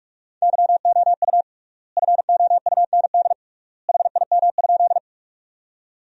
Morse code, heard as a keyed tone, about 47 wpm, 700 Hz. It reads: YOU FOUND HIM?
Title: 3x Word-Spacing - Sentences from Top 300 Words - 45wpm